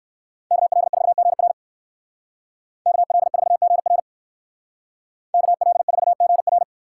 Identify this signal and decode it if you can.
Morse code, heard as a keyed tone, about 58 wpm, 700 Hz. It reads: KL4GR KL4GR KL4GR